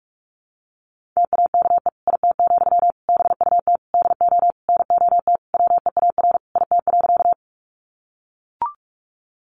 Morse code, heard as a keyed tone, about 44 wpm, 700 Hz. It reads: TAKE IT, BUT DO NOT WEAR IT.
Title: Sentences from Top 1500 Words 45wpm